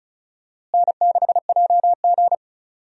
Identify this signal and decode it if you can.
Morse code, heard as a keyed tone, about 36 wpm, 700 Hz. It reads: N6JG
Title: USA Call Signs - 4 Characters - Repeated 2x - 35wpm